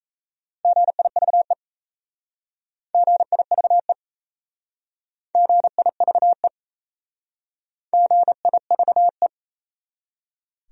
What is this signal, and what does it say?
Morse code, heard as a keyed tone, about 34 wpm, 690 Hz. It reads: GIVE GIVE GIVE GIVE
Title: Speed-Racing Top 500 Words 28wpm